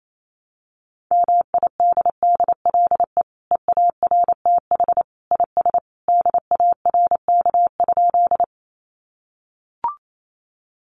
Morse code, heard as a keyed tone, about 28 wpm, 700 Hz. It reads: MIDDLE EARTH IS DARK?
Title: Sentences from Top 900 Words 28wpm